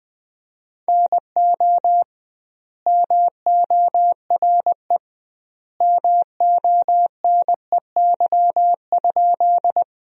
Morse code, heard as a keyed tone, about 20 wpm, 700 Hz. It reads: NO MORE MONEY?
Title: 2x Word-Spacing - Sentences from Top 400 Words - 20wpm